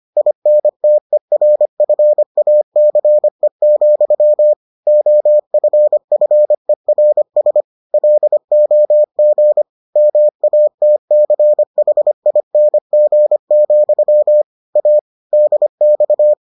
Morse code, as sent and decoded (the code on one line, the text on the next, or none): .. -. - . .-. ..-. .- -.-. --..-- / --- ..-. ..-. .-. ... / .-.. --- --. / -- .- - -.-. .... .. -. --. --..-- / .- / -.. -..-
INTERFACE, OFFERS LOG MATCHING, A DX